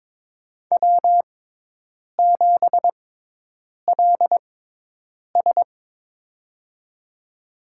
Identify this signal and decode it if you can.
Morse code, heard as a keyed tone, about 23 wpm, spaced as for 8 wpm, 700 Hz. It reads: W7LS
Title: USA Call Signs - 4 Characters - 6x Character Spacing - 22wpm